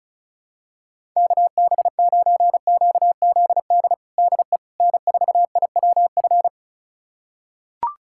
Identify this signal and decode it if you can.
Morse code, heard as a keyed tone, about 36 wpm, 700 Hz. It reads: KB9QZD DE N4IWF
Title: USA Callsign DE Callsign 35wpm